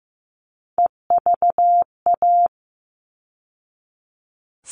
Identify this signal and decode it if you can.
Morse code, heard as a keyed tone, about 15 wpm, 700 Hz. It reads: EVA
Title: All Cut Numbers - 3-Digit 15wpm